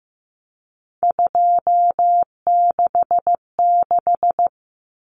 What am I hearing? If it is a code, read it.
Morse code, heard as a keyed tone, about 15 wpm, 700 Hz. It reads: 266
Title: Common Cut Numbers - 3-Digit 15wpm